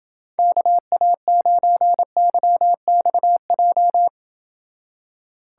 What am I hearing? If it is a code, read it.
Morse code, heard as a keyed tone, about 29 wpm, 700 Hz. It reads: KA9YXJ